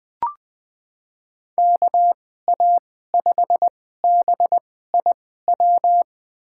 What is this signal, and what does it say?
Morse code, heard as a keyed tone, about 20 wpm, spaced as for 14 wpm, 700 Hz. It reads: KA5BIW
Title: USA Call Signs - 6 Characters - 2x Character Spacing - 20wpm